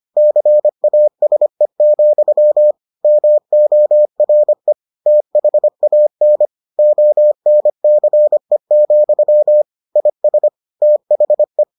Morse code, heard as a keyed tone, about 25 wpm, 600 Hz. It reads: CASE, MORE THAN ONCE, IS THE